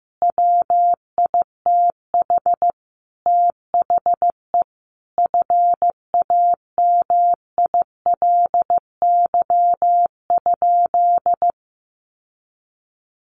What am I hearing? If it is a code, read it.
Morse code, heard as a keyed tone, about 14 wpm, 700 Hz. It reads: WITH THE FAMILY?
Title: Sentences from Top 1100 Words 15wpm